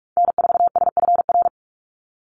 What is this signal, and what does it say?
Morse code, heard as a keyed tone, about 46 wpm, 700 Hz. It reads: N4SFR
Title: NA SOTA Activator Callsigns 45wpm